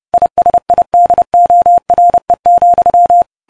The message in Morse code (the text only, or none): ISIDORE,